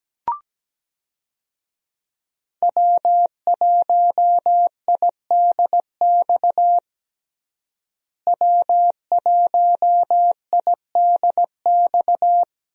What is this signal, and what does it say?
Morse code, heard as a keyed tone, about 17 wpm, 700 Hz. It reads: W1IDX W1IDX
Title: USA Call Signs - 5 Characters - Repeated 2x - 17wpm